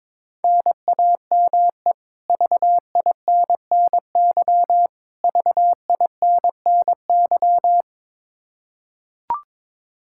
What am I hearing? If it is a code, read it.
Morse code, heard as a keyed tone, about 22 wpm, 700 Hz. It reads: NAME VINNY VINNY
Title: Practice QSO - Exchange 1 - Beginning - 22wpm